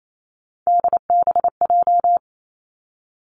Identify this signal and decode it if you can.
Morse code, heard as a keyed tone, about 28 wpm, 700 Hz. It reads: DBJ
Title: Three Challenging Letter-Number Combinations 28wpm